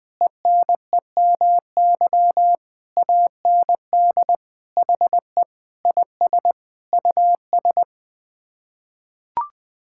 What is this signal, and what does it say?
Morse code, heard as a keyed tone, about 20 wpm, 700 Hz. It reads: ENEMY AND HE IS US
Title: Sets of 9 Words - 20wpm